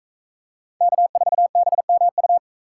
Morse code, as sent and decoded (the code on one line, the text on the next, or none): -.- ....- -... -- ..-
K4BMU